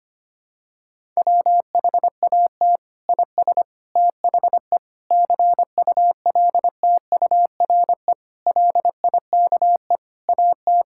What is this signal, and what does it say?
Morse code, heard as a keyed tone, about 25 wpm, 700 Hz. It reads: WHAT IS THE CULTURE LIKE AT
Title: Sentences from Top 1000 Words - Review Part 6 - 25wpm